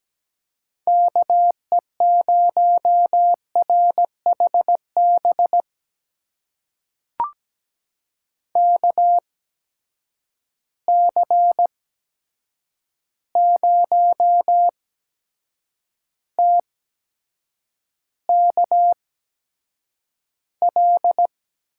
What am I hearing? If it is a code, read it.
Morse code, heard as a keyed tone, about 17 wpm, spaced as for 7 wpm, 700 Hz. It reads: KE0RHB KC0TKL
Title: USA Call Signs - 6 Characters - 8x Character Spacing - 17wpm